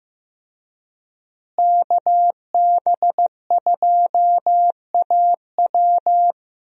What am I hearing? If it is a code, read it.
Morse code, heard as a keyed tone, about 15 wpm, 700 Hz. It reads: KB2AW
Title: USA Call Signs - 5 Characters - Repeated 3x - 15wpm